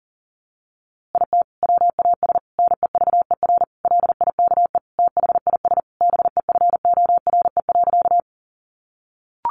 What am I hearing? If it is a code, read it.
Morse code, heard as a keyed tone, about 40 wpm, 700 Hz. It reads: IT WAS NEVER LIKE THIS BEFORE.
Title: A Sound Of Thunder 40wpm